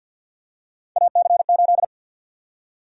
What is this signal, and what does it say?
Morse code, heard as a keyed tone, about 51 wpm, 700 Hz. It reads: AC8
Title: USA Call Sign Prefixes - Repeated 2x - 50wpm